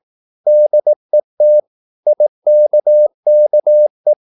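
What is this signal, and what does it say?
Morse code, heard as a keyed tone, about 18 wpm, 600 Hz. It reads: DET IKKE
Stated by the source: Slow (QRS) Morse practice at 18 wpm, CW med Gnister.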